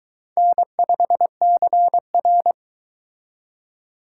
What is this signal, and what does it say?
Morse code, heard as a keyed tone, about 23 wpm, 700 Hz. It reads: N5CR